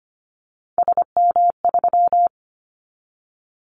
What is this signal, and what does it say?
Morse code, heard as a keyed tone, about 25 wpm, 700 Hz. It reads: SM3
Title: Callsign Prefixes for Top 25 DXCC - 25wpm